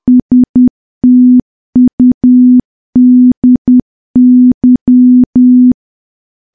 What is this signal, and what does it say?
Morse code, heard as a keyed tone, about 10 wpm, 250 Hz. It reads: STUDY
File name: morse / morse-training-10WPM